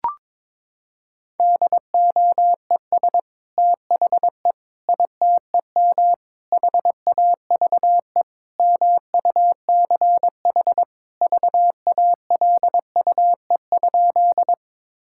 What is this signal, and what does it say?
Morse code, heard as a keyed tone, about 22 wpm, 700 Hz. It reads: DOES THE ITEM HAVE MUCH VALUE?
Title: Sentences from Top 1000 Words - Review Part 6 - 22wpm